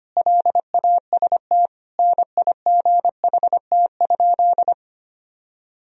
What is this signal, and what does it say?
Morse code, heard as a keyed tone, about 25 wpm, 700 Hz. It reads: LAST NIGHT?
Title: Sentences from Top 1000 Words - Review Part 2 - 25wpm